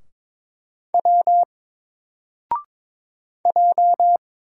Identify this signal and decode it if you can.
Morse code, heard as a keyed tone, about 22 wpm, 700 Hz. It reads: W J
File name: Single-Letter-Number-22wpm